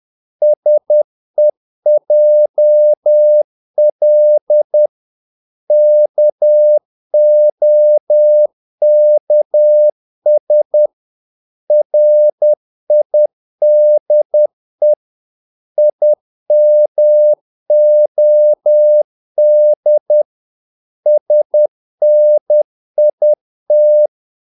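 Morse code, heard as a keyed tone, about 10 wpm, 600 Hz. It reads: SEJL KOKS RIDE IMOD SNIT
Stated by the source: Morse practice of short Danish words at 10 wpm, from CW med Gnister.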